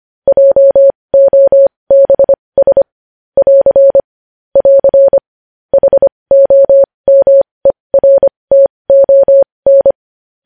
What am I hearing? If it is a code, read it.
Morse code, heard as a keyed tone, about 25 wpm, spaced as for 20 wpm, 550 Hz. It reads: JOBS <AR> <AR> HOMERTON
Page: news-headlines-in-morse-code-at-20-wpm